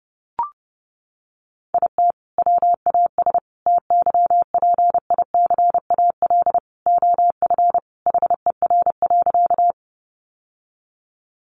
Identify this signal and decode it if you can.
Morse code, heard as a keyed tone, about 30 wpm, 700 Hz. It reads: IT WAS TYPICAL OF HER.
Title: Sentences from Top 1100 Words 30wpm